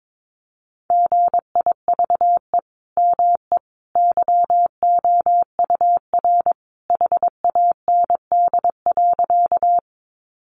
Morse code, heard as a keyed tone, about 22 wpm, 700 Hz. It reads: GIVE ME YOUR HAND.
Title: Sentences from Top 200 Words 22wpm